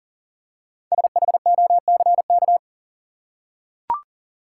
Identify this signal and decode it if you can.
Morse code, heard as a keyed tone, about 40 wpm, 700 Hz. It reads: SHOCK
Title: Five Letter Words - 40wpm